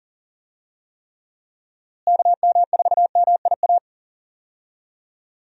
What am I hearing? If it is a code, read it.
Morse code, heard as a keyed tone, about 40 wpm, 700 Hz. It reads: KM4MIA
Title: USA Call Signs - 6 Characters - Repeated 3x - 40wpm